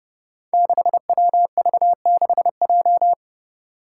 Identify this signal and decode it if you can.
Morse code, heard as a keyed tone, about 30 wpm, 700 Hz. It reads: 6WV6J
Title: Five Challenging Letter-Number Combinations 30wpm